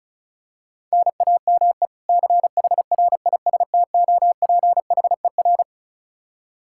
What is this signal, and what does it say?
Morse code, heard as a keyed tone, about 34 wpm, 700 Hz. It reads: NAME CHRISTOPHER
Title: Practice QSO - Exchange 1 - Beginning - 35wpm